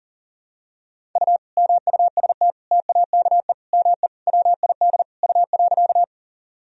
Morse code, heard as a keyed tone, about 40 wpm, 700 Hz. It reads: U MUST TAKE ME WID U.
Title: Sentences from Top 200 Words and Abbreviations 40wpm